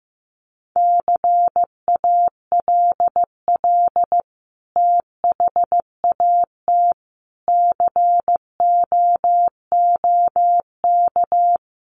Morse code, heard as a keyed tone, about 15 wpm, 700 Hz. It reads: CALL THAT COOK
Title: Sentences from Top 1100 Words 15wpm